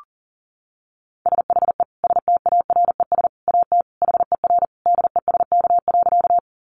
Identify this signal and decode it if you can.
Morse code, heard as a keyed tone, about 40 wpm, 700 Hz. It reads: SHE STARES AT HER DESK.
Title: Easy Phrases 40wpm